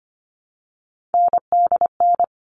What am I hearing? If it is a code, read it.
Morse code, heard as a keyed tone, about 25 wpm, 700 Hz. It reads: NDN
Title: All Cut Numbers - 3-Digit 25wpm